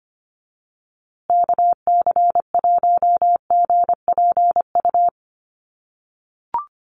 Morse code, heard as a keyed tone, about 25 wpm, 700 Hz. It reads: KC1GPU
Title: USA Call Signs Repeated 2x 25wpm